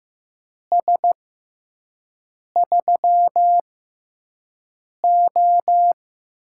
Morse code, heard as a keyed tone, about 15 wpm, spaced as for 5 wpm, 700 Hz. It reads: S3O
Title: USA Call Signs - 4 Characters - 6x Character Spacing - 15wpm